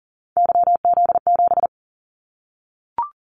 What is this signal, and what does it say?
Morse code, heard as a keyed tone, about 37 wpm, 700 Hz. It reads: YZ7
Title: Callsign Prefixes for Top 25 DXCC - Repeated 3x - 40wpm